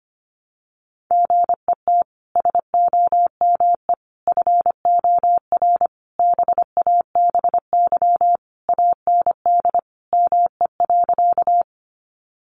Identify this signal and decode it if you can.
Morse code, heard as a keyed tone, about 25 wpm, 700 Hz. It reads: GET SOME FOR BABY AND ME.
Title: Sentences from Top 1300 Words 25wpm